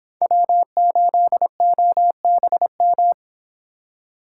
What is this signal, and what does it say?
Morse code, heard as a keyed tone, about 25 wpm, 700 Hz. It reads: W8OBM